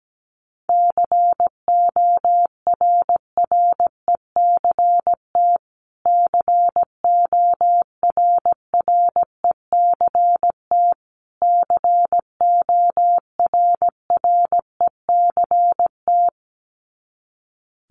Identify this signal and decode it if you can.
Morse code, heard as a keyed tone, about 17 wpm, 700 Hz. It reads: CORRECT CORRECT CORRECT
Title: Top 500 Words Repeated 17wpm